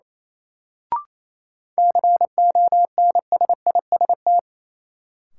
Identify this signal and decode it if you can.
Morse code, heard as a keyed tone, about 28 wpm, 700 Hz. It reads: CONSIST
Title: Sentences from Top 1400 Words - New Word Drill - 28wpm